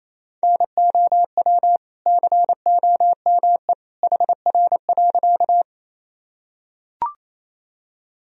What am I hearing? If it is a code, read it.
Morse code, heard as a keyed tone, about 28 wpm, 700 Hz. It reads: NOW COME HR.
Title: Sentences from Top 200 Words and Abbreviations 28wpm